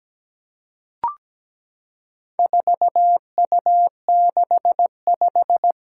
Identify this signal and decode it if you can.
Morse code, heard as a keyed tone, about 17 wpm, 700 Hz. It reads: 4U65
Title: Four Challenging Letter-Number Combinations 17wpm